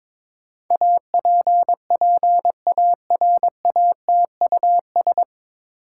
Morse code, heard as a keyed tone, about 23 wpm, 700 Hz. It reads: APPARATUS
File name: 9 Letter Words - First 3 Letters Given - 22wpm